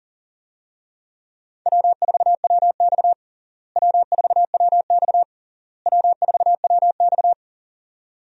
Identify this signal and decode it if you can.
Morse code, heard as a keyed tone, about 40 wpm, 700 Hz. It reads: W4WX W4WX W4WX
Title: USA Call Signs - 4 Characters - Repeated 3x - 40wpm